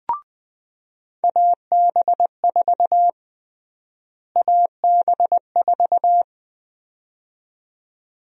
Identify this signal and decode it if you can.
Morse code, heard as a keyed tone, about 20 wpm, 700 Hz. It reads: AB4 AB4
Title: USA Call Sign Prefixes - Repeated 2x - 20wpm